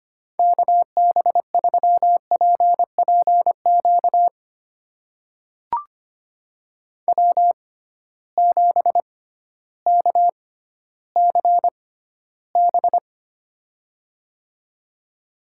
Morse code, heard as a keyed tone, about 25 wpm, spaced as for 13 wpm, 700 Hz. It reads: KB3PPQ W7KCB
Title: USA Call Signs - 6x Character Spacing - 25wpm